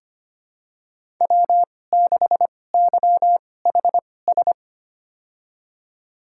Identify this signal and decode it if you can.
Morse code, heard as a keyed tone, about 25 wpm, spaced as for 18 wpm, 700 Hz. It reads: W6YHS
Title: USA Call Signs - 5 Characters - 2x Character Spacing - 25wpm